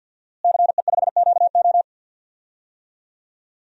Morse code, heard as a keyed tone, about 51 wpm, 700 Hz. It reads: CE5QO
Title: DXCC-100 Call Signs - 5 Characters - Repeated 2x - 50wpm